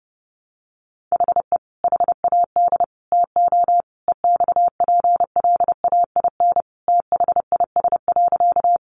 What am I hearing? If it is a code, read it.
Morse code, heard as a keyed tone, about 30 wpm, 700 Hz. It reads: HE HAD TO EXPLAIN THIS.
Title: Sentences from Top 1100 Words 30wpm